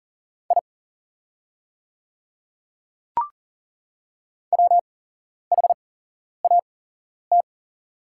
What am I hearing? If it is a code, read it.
Morse code, heard as a keyed tone, about 40 wpm, spaced as for 11 wpm, 690 Hz. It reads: I WHAT
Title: 8x Character Spacing - Top 100 Words - 40wpm